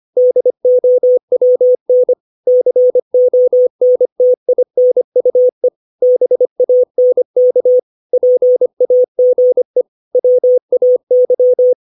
Morse code, heard as a keyed tone, about 25 wpm, 500 Hz. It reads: DOWN CONTINUE BANK PAGE WAY